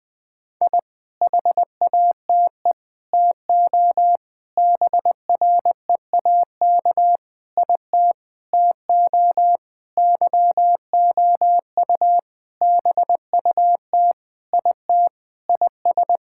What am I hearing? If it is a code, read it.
Morse code, heard as a keyed tone, about 20 wpm, 700 Hz. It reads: I HATE TO BREAK IT TO YOU BUT IT IS
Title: Sentences with 8-Word Phrases - 20wpm